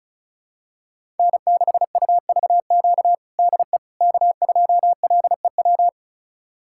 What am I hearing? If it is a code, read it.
Morse code, heard as a keyed tone, about 36 wpm, 700 Hz. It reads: N6UVQ DE K2LEW
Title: USA Callsign DE Callsign - 5 Characters - 35wpm